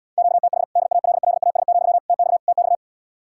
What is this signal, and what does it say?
Morse code, heard as a keyed tone, about 79 wpm, 700 Hz. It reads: QTH DALLAS, TX TX